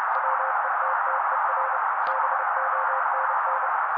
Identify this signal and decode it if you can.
Morse code, heard as a keyed tone, about 30 wpm, 530 Hz. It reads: WETTER BONN
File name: daily / 29wpm noise